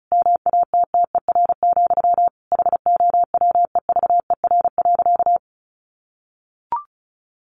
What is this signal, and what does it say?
Morse code, heard as a keyed tone, about 34 wpm, 700 Hz. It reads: MATTER, HOWEVER.
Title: Sentences from Top 700 Words 35wpm